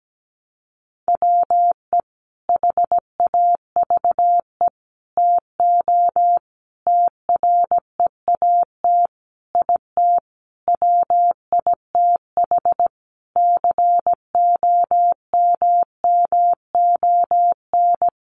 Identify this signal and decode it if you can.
Morse code, heard as a keyed tone, about 17 wpm, 700 Hz. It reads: WE HAVE TO TREAT IT WITH COMMON
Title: Sentences from Top 1300 Words 17wpm